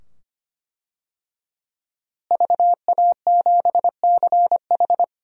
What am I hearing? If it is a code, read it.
Morse code, heard as a keyed tone, about 25 wpm, 700 Hz. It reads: VA7CH
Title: DXCC-100 Call Signs - 5 Characters - Repeated 2x - 25wpm